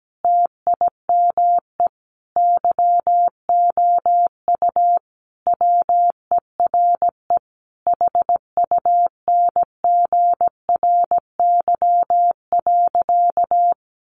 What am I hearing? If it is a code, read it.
Morse code, heard as a keyed tone, about 17 wpm, 700 Hz. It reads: TIME YOU WERE HUNGRY.
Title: Sentences from Top 1400 Words 17wpm